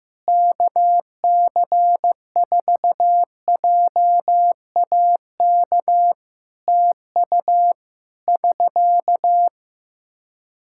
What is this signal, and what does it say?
Morse code, heard as a keyed tone, about 15 wpm, 700 Hz. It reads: KC4JAK TU <SK>
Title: Practice QSO - Exchange 4 - Ending - 15wpm